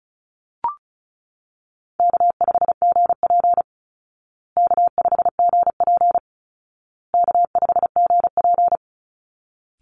Morse code, heard as a keyed tone, about 34 wpm, 700 Hz. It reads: K5GP K5GP K5GP
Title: Easy Call Signs Repeated 3x - 35wpm